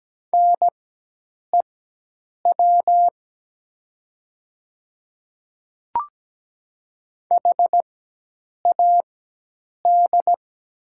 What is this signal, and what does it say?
Morse code, heard as a keyed tone, about 17 wpm, spaced as for 8 wpm, 700 Hz. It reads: NEW HAD
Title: 4x Character Spacing - Top 200 Words - 17wpm